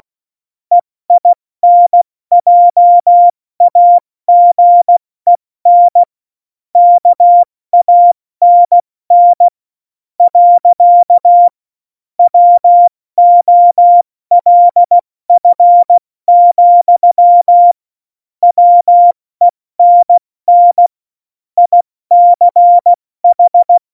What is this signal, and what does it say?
Morse code, heard as a keyed tone, about 16 wpm, spaced as for 14 wpm, 700 Hz. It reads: EINJAGEN KANN. WOLF, WENN ICH
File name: Morsekurs / Witze / Witz 18. Dezember, 70 F80